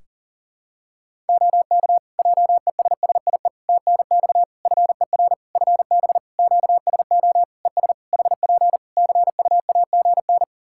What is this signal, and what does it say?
Morse code, heard as a keyed tone, about 40 wpm, 700 Hz. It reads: OK JESSIE TNX FER FB QSO ES HP CUAGN
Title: Practice QSO - Exchange 4 - Ending - 40wpm